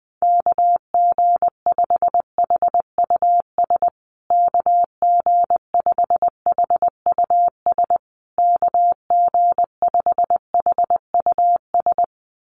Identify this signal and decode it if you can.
Morse code, heard as a keyed tone, about 20 wpm, 700 Hz. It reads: KG5HUS KG5HUS KG5HUS
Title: USA Call Signs Repeated 3x 20wpm